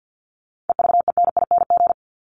Morse code, heard as a keyed tone, about 50 wpm, 700 Hz. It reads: EVENING
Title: Top 1501-1750 Words 50wpm